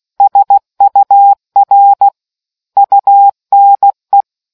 Morse code, heard as a keyed tone, about 16 wpm, 800 Hz. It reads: SUR UNE